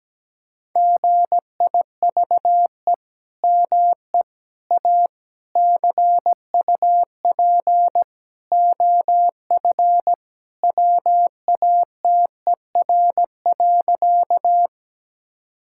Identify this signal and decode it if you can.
Morse code, heard as a keyed tone, about 17 wpm, 700 Hz. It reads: GIVE ME A CUP OF WATER.
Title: Sentences from Top 1600 Words 17wpm